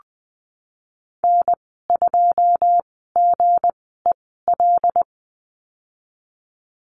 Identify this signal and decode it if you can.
Morse code, heard as a keyed tone, about 20 wpm, spaced as for 15 wpm, 700 Hz. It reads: N2GEL
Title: 161 Lesson 26g - Letter g - Call Signs 20wpm